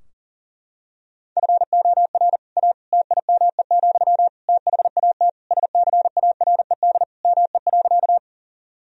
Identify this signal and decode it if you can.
Morse code, heard as a keyed tone, about 40 wpm, 700 Hz. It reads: FOR A TIME, THAT SCARED ME.
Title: Sentences from Top 1000 Words - Review Part 4 - 40wpm